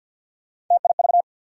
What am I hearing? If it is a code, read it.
Morse code, heard as a keyed tone, about 47 wpm, 700 Hz. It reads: TIV
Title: USA Call Sign Postfixes - 3 Letters 50wpm